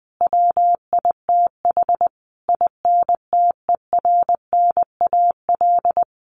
Morse code, heard as a keyed tone, about 20 wpm, 700 Hz. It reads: WITH INTERNAL
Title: Sentences from Top 1400 Words 20wpm